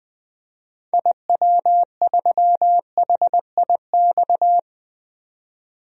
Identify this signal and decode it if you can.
Morse code, heard as a keyed tone, about 20 wpm, 700 Hz. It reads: IW3HIX
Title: DXCC-100 Call Signs - 6 Characters - 20wpm